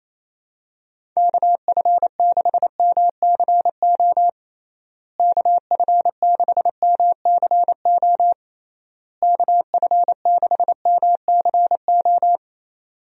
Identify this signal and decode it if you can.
Morse code, heard as a keyed tone, about 28 wpm, 700 Hz. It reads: KF6MCO KF6MCO KF6MCO
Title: USA Call Signs - 6 Characters - Repeated 3x - 28wpm